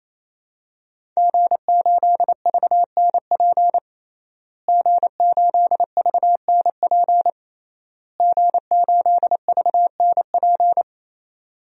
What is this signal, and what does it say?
Morse code, heard as a keyed tone, about 27 wpm, 700 Hz. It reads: G8VNP G8VNP G8VNP
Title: DXCC-100 Call Signs - 5 Characters - Repeated 3x - 28wpm